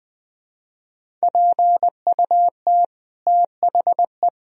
Morse code, heard as a keyed tone, about 20 wpm, 700 Hz. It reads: PUT THE
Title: Sentences from Top 1000 Words - Review Part 3 - 20wpm